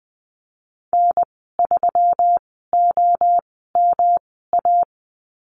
Morse code, heard as a keyed tone, about 20 wpm, spaced as for 14 wpm, 700 Hz. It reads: N3OMA
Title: 147 Lesson 24g - Letter y - Call Signs 20wpm